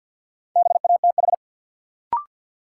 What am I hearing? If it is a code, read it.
Morse code, heard as a keyed tone, about 52 wpm, 700 Hz. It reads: BATH
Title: Sentences from Top 1600 Words - New Word Drill Alt - 50wpm